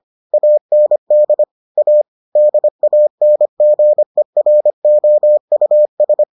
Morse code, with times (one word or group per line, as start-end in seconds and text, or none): 0.3-1.4 AND
1.8-2.0 A
2.4-6.2 DANGEROUS